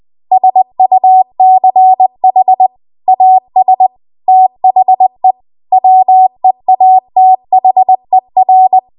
Morse code, heard as a keyed tone, about 20 wpm, 750 Hz. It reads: SUCH AS THE WEATHER
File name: Archive / 20 WPM / 20WPM